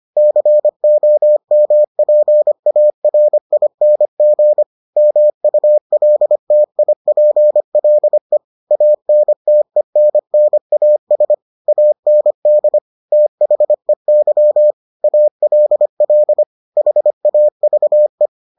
Morse code, heard as a keyed tone, about 25 wpm, 600 Hz. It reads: COMPARING MULTIPLE ANTENNAS AND THEY ALL HAVE